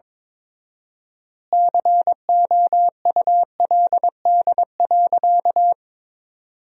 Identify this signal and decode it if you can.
Morse code, heard as a keyed tone, about 22 wpm, 700 Hz. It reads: COULD.
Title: 4x Word-Spacing - Sentences from Top 500 Words - 22wpm